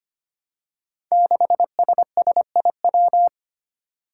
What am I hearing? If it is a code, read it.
Morse code, heard as a keyed tone, about 25 wpm, 700 Hz. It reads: 6SSIW
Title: Code Groups - 5 Letter:Numbers - 25wpm